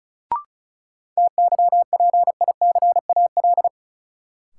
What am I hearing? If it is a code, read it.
Morse code, heard as a keyed tone, about 34 wpm, 700 Hz. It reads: TYPICAL
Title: Sentences from Top 1100 Words - New Word Drill Alt - 35wpm